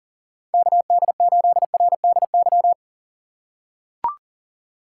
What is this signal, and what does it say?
Morse code, heard as a keyed tone, about 40 wpm, 700 Hz. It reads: KD8RDY
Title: USA Call Signs - 4x Character Spacing - 40wpm